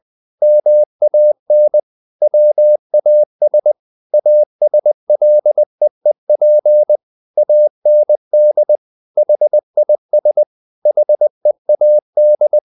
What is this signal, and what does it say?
Morse code, heard as a keyed tone, about 20 wpm, 600 Hz. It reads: MAN WAS ASLEEP AND HIS HEAD
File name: the old man and the sea / the old man and the sea part 1 20wpm